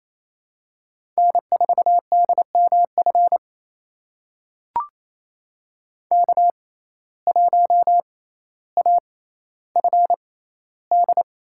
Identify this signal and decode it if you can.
Morse code, heard as a keyed tone, about 28 wpm, spaced as for 14 wpm, 700 Hz. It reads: N4DMF K1AFD
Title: USA Call Signs - 5 Characters - 6x Character Spacing - 28wpm